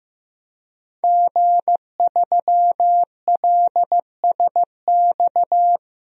Morse code, heard as a keyed tone, about 15 wpm, 700 Hz. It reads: G3LSX
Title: DXCC-100 Call Signs - 5 Characters - 15wpm